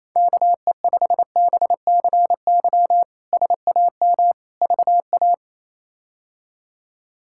Morse code, heard as a keyed tone, about 28 wpm, 700 Hz. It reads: KE5BCY SAM VA
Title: K1USN SST QSO Practice - 28wpm